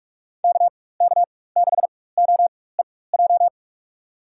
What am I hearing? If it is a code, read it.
Morse code, heard as a keyed tone, about 46 wpm, spaced as for 21 wpm, 700 Hz. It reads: KK6OEJ
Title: USA Call Signs - 4x Character Spacing - 45wpm